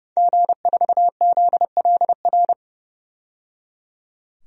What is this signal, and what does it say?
Morse code, heard as a keyed tone, about 30 wpm, 700 Hz. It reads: G4ZLR
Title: DXCC-100 Call Signs - 5 Characters - 30wpm